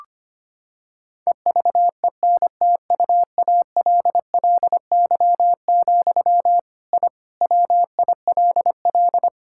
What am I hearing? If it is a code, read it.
Morse code, heard as a keyed tone, about 25 wpm, 700 Hz. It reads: EVENTUALLY, I WILL
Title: Sentences from Top 1000 Words - Review Part 6 - 25wpm